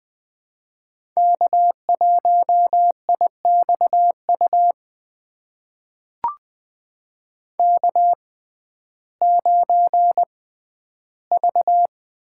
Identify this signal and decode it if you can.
Morse code, heard as a keyed tone, about 20 wpm, spaced as for 12 wpm, 700 Hz. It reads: K1IXU K9V
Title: USA Call Signs - 5 Characters - 6x Character Spacing - 20wpm